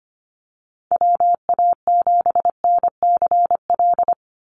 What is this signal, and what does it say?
Morse code, heard as a keyed tone, about 25 wpm, 690 Hz. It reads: WA7NCL
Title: NA SOTA Activator Callsigns 25wpm